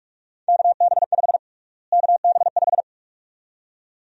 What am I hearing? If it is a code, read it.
Morse code, heard as a keyed tone, about 46 wpm, 700 Hz. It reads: KB5 KB5
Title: USA Call Sign Prefixes - Repeated 2x - 45wpm